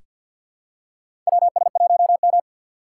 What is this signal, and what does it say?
Morse code, heard as a keyed tone, about 47 wpm, 690 Hz. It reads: WS1M
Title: USA Call Signs - 4 Characters - 6x Character Spacing - 50wpm